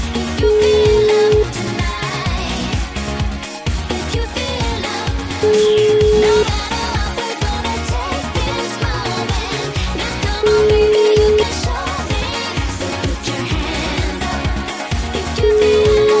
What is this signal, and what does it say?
Ringback tone (Europe, 425 Hz), with noise.